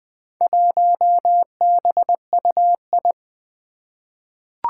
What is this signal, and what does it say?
Morse code, heard as a keyed tone, about 20 wpm, 700 Hz. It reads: 1BUI
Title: Four Challenging Letter-Number Combinations 20wpm